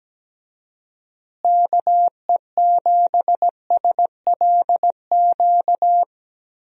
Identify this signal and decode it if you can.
Morse code, heard as a keyed tone, about 17 wpm, 700 Hz. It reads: KE7SLQ
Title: USA Call Signs - 6 Characters - Repeated 3x - 17wpm